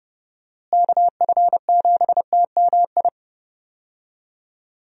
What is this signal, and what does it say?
Morse code, heard as a keyed tone, about 30 wpm, 700 Hz. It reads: KF7TMI
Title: USA Call Signs - 6 Characters - Repeated 3x - 30wpm